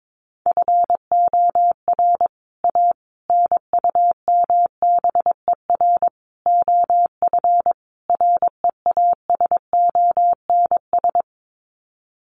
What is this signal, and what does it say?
Morse code, heard as a keyed tone, about 22 wpm, 700 Hz. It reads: FOR A NUMBER OF REASONS
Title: Sets of 5 Words Encore2 22wpm